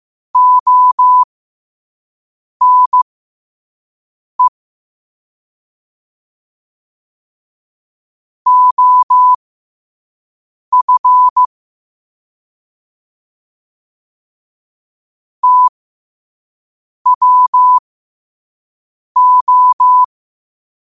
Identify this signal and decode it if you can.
Morse code, heard as a keyed tone, about 15 wpm, spaced as for 5 wpm, 1000 Hz. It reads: ONE OF TWO
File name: qotdfiles / 5wpm